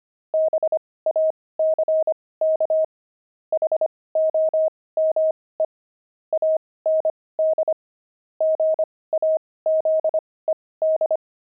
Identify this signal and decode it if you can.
Morse code, heard as a keyed tone, about 25 wpm, spaced as for 18 wpm, 630 Hz. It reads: BACK HOME AND GAZED
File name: pangram-fs-18-cs-25-jay